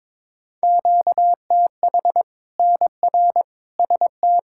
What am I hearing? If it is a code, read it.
Morse code, heard as a keyed tone, about 22 wpm, 700 Hz. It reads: QTH NR ST
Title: K1USN SST QSO Practice 22wpm